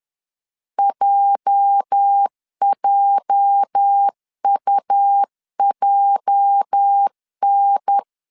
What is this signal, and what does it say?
Morse code, heard as a keyed tone, about 10 wpm, 800 Hz. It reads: JJUJN
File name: LEZIONI / 8 YJ45 / 8 YJ 10wpm 2651